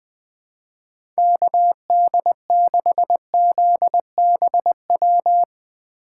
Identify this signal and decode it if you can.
Morse code, heard as a keyed tone, about 20 wpm, 700 Hz. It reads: KD6ZBW
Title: USA Call Signs - 4x Character Spacing - 20wpm